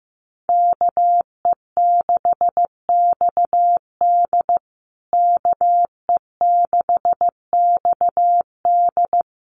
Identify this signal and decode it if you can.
Morse code, heard as a keyed tone, about 15 wpm, 700 Hz. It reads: KE6XD KE6XD
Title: USA Call Signs Repeated 3x 15wpm